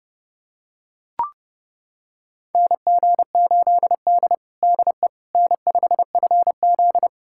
Morse code, heard as a keyed tone, about 30 wpm, 700 Hz. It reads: NG8D DE N5FZ
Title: USA Callsign DE Callsign - 4 Characters - 30wpm